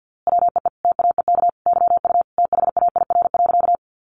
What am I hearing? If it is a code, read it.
Morse code, heard as a keyed tone, about 50 wpm, 700 Hz. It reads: WEE TAEK YU THAIR.
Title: A Sound Of Thunder 50wpm